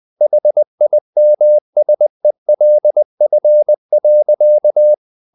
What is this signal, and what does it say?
Morse code, heard as a keyed tone, about 20 wpm, 600 Hz. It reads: HIMSELF.
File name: the old man and the sea / the old man and the sea part 1 20wpm